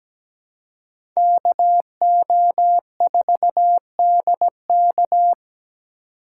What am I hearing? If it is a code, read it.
Morse code, heard as a keyed tone, about 17 wpm, 700 Hz. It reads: KO4DK
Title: USA Call Signs - 5 Characters - 3x Character Spacing - 17wpm